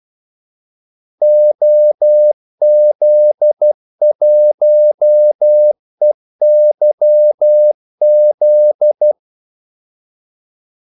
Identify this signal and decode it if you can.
Morse code, heard as a keyed tone, about 12 wpm, 600 Hz. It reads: OZ1EYZ
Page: 2 ekstra ordmellemrum mellem kaldesignalerne